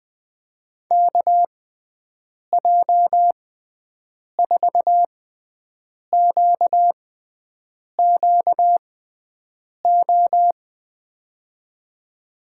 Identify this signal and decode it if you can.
Morse code, heard as a keyed tone, about 20 wpm, spaced as for 7 wpm, 700 Hz. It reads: KJ4QQO
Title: USA Call Signs - 6 Characters - 6x Character Spacing - 20wpm